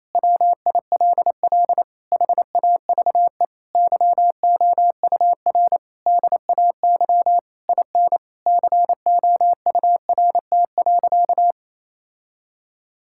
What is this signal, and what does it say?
Morse code, heard as a keyed tone, about 28 wpm, 700 Hz. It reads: WILL HAVE YOUR DAY IN COURT.
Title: Sentences from Top 1600 Words 28wpm